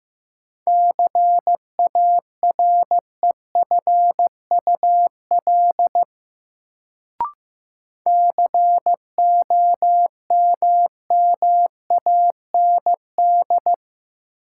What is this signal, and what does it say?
Morse code, heard as a keyed tone, about 15 wpm, 700 Hz. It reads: CAREFUL COMMAND
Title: Sentences from Top 1600 Words - New Word Drill Alt - 15wpm